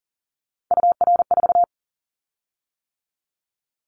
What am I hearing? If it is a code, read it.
Morse code, heard as a keyed tone, about 39 wpm, 700 Hz. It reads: UR4
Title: Callsign Prefixes for Top 25 DXCC - 40wpm